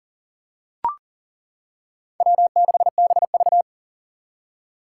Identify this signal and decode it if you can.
Morse code, heard as a keyed tone, about 38 wpm, 690 Hz. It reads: W6BV
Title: Four Challenging Letter-Number Combinations 40wpm